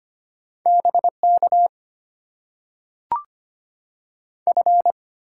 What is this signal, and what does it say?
Morse code, heard as a keyed tone, about 25 wpm, 700 Hz. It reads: BK F
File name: Top 90 QSO Elements - 6x Character Spacing - 25wpm